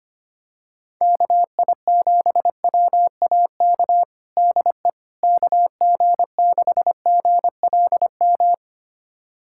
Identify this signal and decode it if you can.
Morse code, heard as a keyed tone, about 25 wpm, 700 Hz. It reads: KI7WAK DE KG6GLM
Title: USA Callsign DE Callsign - 6 Characters - 25wpm